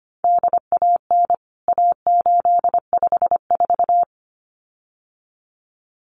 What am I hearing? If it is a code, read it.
Morse code, heard as a keyed tone, about 25 wpm, 700 Hz. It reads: DAN A854
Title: CWT Exchange 25wpm